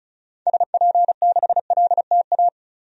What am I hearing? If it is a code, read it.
Morse code, heard as a keyed tone, about 36 wpm, 700 Hz. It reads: SP6LTA
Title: DXCC-100 Call Signs - 6 Characters - Repeated 3x - 35wpm